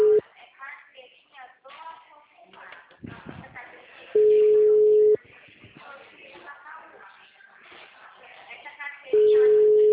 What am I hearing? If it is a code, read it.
Ringback tone (Europe, 425 Hz), with speech.
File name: out- -2022- - - .69309